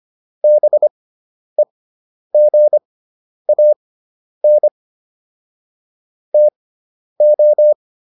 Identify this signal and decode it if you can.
Morse code, heard as a keyed tone, about 25 wpm, spaced as for 10 wpm, 600 Hz. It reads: BEGAN TO